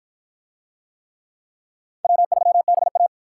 Morse code, heard as a keyed tone, about 48 wpm, 700 Hz. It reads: W3BA